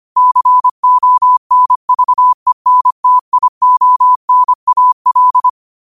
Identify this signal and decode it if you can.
Morse code, heard as a keyed tone, about 25 wpm, 1000 Hz. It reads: CONVENTIONAL
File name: qotdfiles / 25wpm